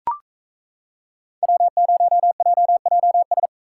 Morse code, heard as a keyed tone, about 33 wpm, 700 Hz. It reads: W0JJS
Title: Speed Racing - USA 1x3 Call Signs 28wpm